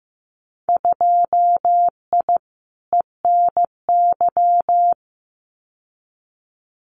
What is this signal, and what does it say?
Morse code, heard as a keyed tone, about 15 wpm, 700 Hz. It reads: 2I ENY
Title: Winter Field Day Exchanges Encore 15wpm